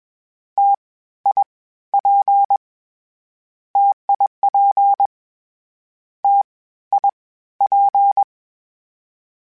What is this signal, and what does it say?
Morse code, heard as a keyed tone, about 21 wpm, 800 Hz. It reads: T I P TIP T I P
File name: Difficult Combos / Difficult Combos-mp3s / T AOEI @20